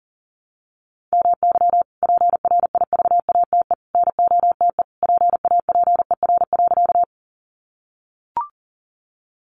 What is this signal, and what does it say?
Morse code, heard as a keyed tone, about 40 wpm, 700 Hz. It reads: MY PRIVATE NOTE PAPER.
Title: Sentences from Top 1000 Words 40wpm